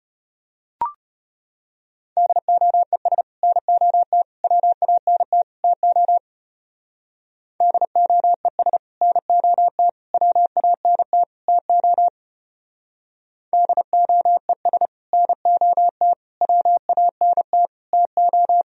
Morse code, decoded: DOES NOT WANT TO DOES NOT WANT TO DOES NOT WANT TO

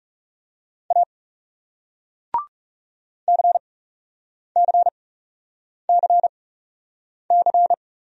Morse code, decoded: A CCCC

